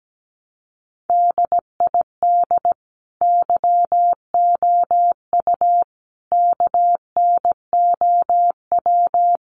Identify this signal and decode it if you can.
Morse code, heard as a keyed tone, about 17 wpm, 700 Hz. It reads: DID YOU KNOW
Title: Sentences from Top 100 Words 17wpm